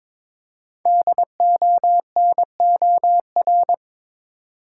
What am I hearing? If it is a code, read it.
Morse code, heard as a keyed tone, about 22 wpm, 700 Hz. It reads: DONOR
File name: 5 Letter Words - First 3 Letters Given - 22wpm